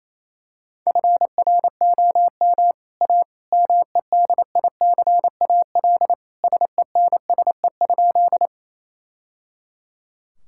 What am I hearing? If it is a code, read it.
Morse code, heard as a keyed tone, about 28 wpm, 700 Hz. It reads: FROM A MEDICAL SENSE?
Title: Sentences from Top 1000 Words - Review Part 5 - 28wpm